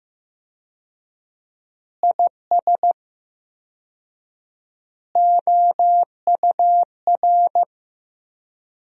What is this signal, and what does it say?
Morse code, heard as a keyed tone, about 15 wpm, 700 Hz. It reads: IS OUR
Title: 4x Word-Spacing - Sentences from Top 500 Words - 15wpm